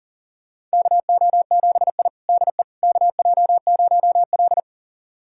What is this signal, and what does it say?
Morse code, heard as a keyed tone, about 40 wpm, 700 Hz. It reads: KO7I DE KJ0L
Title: USA Callsign DE Callsign - 4 Characters - 40wpm